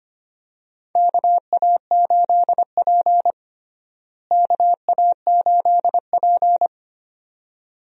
Morse code, heard as a keyed tone, about 25 wpm, 700 Hz. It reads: KA8P KA8P
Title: USA Call Signs - 4 Characters - Repeated 2x - 25wpm